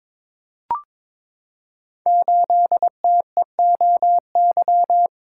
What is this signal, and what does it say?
Morse code, heard as a keyed tone, about 22 wpm, 700 Hz. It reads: 8TEOY
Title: Code Groups - 5 Letter:Numbers - 22wpm